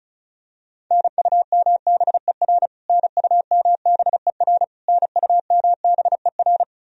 Morse code, heard as a keyed tone, about 36 wpm, 700 Hz. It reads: NUMBER NUMBER NUMBER
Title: Top 100 Words Repeated 35wpm